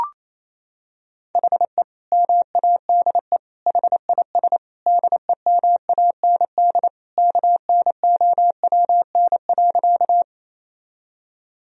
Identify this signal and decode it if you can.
Morse code, heard as a keyed tone, about 28 wpm, 700 Hz. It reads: HE MADE HIS DEMAND KNOWN.
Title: Sentences from Top 1000 Words - Review Part 4 - 28wpm